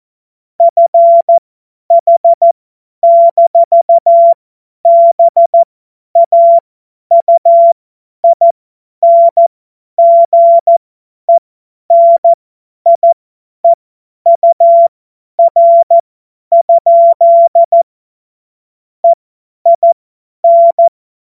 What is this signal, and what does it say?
Morse code, heard as a keyed tone, about 14 wpm, spaced as for 10 wpm, 680 Hz. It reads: FH-BAUINGENIEUR? EIN
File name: Morsekurs / Witze / Witz 31. Mai 2022 50 F70